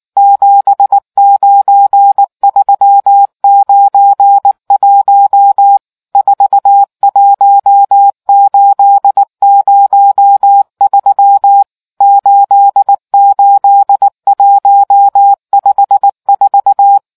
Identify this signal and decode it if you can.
Morse code, heard as a keyed tone, about 19 wpm, 800 Hz. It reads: 79391 41803 88154